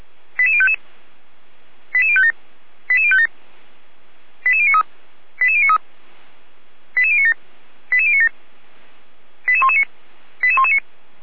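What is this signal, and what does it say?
Five-tone selective call (ZVEI), heard as quick runs of five tones, tones of about 70 ms. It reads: 88055 88057 88057 88093 88093 88097 88097 88119 88119
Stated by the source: ZVEI five-tone selective call received by SDR, recorded by Feuerwehr Orlamünde fire brigade.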